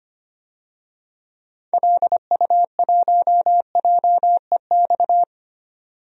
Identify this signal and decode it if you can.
Morse code, heard as a keyed tone, about 25 wpm, 700 Hz. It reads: LU1JEX
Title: DXCC-100 Call Signs - 6 Characters - 25wpm